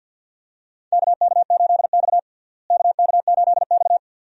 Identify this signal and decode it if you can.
Morse code, heard as a keyed tone, about 51 wpm, 700 Hz. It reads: KK8X KK8X
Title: USA Call Signs - 4 Characters - Repeated 2x - 50wpm